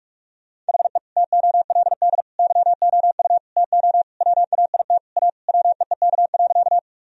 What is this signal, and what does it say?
Morse code, heard as a keyed tone, about 46 wpm, 700 Hz. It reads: HE TOLD YOU TO WAIT A WEEK.
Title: Sentences from Top 1000 Words - Review Part 2 - 45wpm